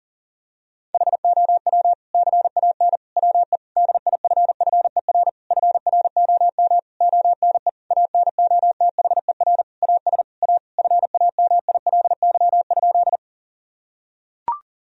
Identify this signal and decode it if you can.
Morse code, heard as a keyed tone, about 40 wpm, 700 Hz. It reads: HOW CAN WE DIFFER FROM ONE ANOTHER AS A FAMILY?
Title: Sentences from Top 1000 Words - Review Part 1 - 40wpm